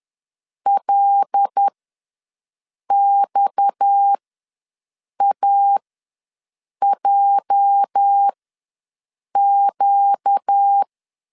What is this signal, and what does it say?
Morse code, heard as a keyed tone, about 10 wpm, spaced as for 6 wpm, 800 Hz. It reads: LXAJQ
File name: LEZIONI / 9 28 ws 10wpm 4124